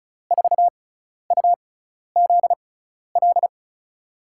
Morse code, heard as a keyed tone, about 36 wpm, spaced as for 12 wpm, 700 Hz. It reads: 4UZL
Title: USA Call Signs - 6x Character Spacing - 35wpm